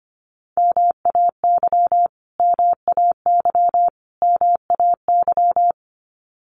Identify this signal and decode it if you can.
Morse code, heard as a keyed tone, about 25 wpm, 700 Hz. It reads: MAY MAY MAY
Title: Top 300 Words Repeated 25wpm